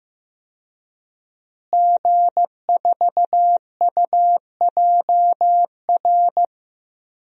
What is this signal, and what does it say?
Morse code, heard as a keyed tone, about 15 wpm, 700 Hz. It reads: G4UJR
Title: DXCC-100 Call Signs - 5 Characters - Repeated 2x - 15wpm